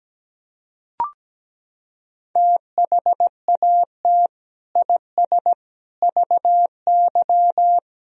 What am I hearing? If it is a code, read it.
Morse code, heard as a keyed tone, about 17 wpm, 700 Hz. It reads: THAT IS VY
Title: Sentences from Top 400 Words and Abbreviations 17wpm